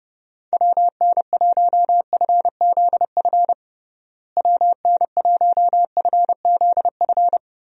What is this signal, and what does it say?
Morse code, heard as a keyed tone, about 29 wpm, 700 Hz. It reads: WN1FZF WN1FZF
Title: USA Call Signs - 6 Characters - Repeated 2x - 30wpm